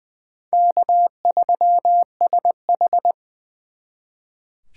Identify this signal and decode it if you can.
Morse code, heard as a keyed tone, about 23 wpm, 690 Hz. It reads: K3SH
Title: Speed Racing - USA 1x2 Call Signs 20wpm